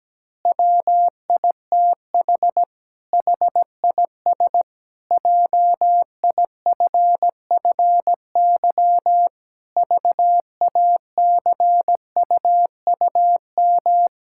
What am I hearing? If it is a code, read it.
Morse code, heard as a keyed tone, about 17 wpm, 700 Hz. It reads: WITH HIS JIFFY VACUUM
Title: Pangram Challenge v2 - 17wpm